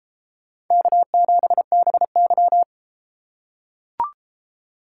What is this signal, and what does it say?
Morse code, heard as a keyed tone, about 31 wpm, 700 Hz. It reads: K7BY